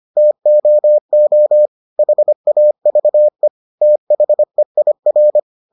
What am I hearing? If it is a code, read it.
Morse code, heard as a keyed tone, about 25 wpm, 600 Hz. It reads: TOO HAVE THEIR